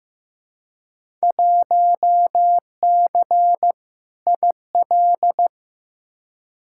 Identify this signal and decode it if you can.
Morse code, heard as a keyed tone, about 15 wpm, 700 Hz. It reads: 1C IL
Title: Field Day Exchanges - 15wpm